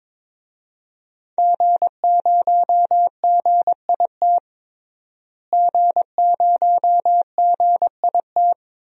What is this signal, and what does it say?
Morse code, heard as a keyed tone, about 22 wpm, 700 Hz. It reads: G0GIT G0GIT
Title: DXCC-100 Call Signs - 5 Characters - Repeated 2x - 22wpm